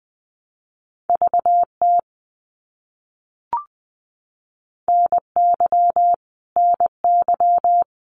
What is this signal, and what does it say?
Morse code, heard as a keyed tone, about 20 wpm, 700 Hz. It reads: VT NY NY
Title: US States Repeated 3x 20wpm